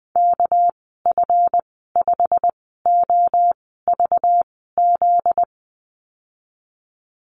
Morse code, heard as a keyed tone, about 20 wpm, spaced as for 14 wpm, 700 Hz. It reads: KF5OVZ
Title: 245 Lesson 39g - Letter z - Call Signs 20wpm